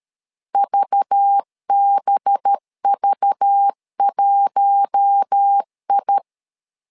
Morse code, heard as a keyed tone, about 13 wpm, 800 Hz. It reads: VBV1I
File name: LEZIONI / 5 LBVC / 5 LBV 12wpm 1855